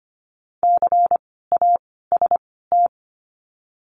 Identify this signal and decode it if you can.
Morse code, heard as a keyed tone, about 25 wpm, 700 Hz. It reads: C A S T
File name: course / 25:10wpm / 094 Lesson 16f - Letter c - Words 25wpm